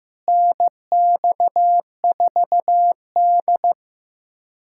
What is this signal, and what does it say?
Morse code, heard as a keyed tone, about 16 wpm, 700 Hz. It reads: NX4D
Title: USA Callsign DE Callsign - 4 Characters - 15wpm